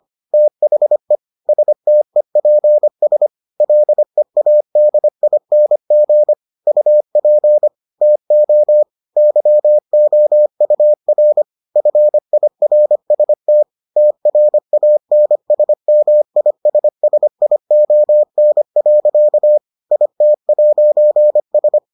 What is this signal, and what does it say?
Morse code, heard as a keyed tone, about 25 wpm, 600 Hz. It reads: THE STEPS LEADING UP TO YOUR FIRST TRANSMISSION. IT'S